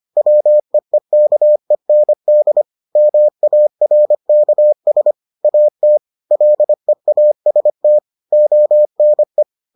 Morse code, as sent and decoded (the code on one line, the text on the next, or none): .-- . -.- . -. -.. / -- .- .-. -.- ... / .- - / .-.. .- ... - / --- -. .
WEEKEND MARKS AT LEAST ONE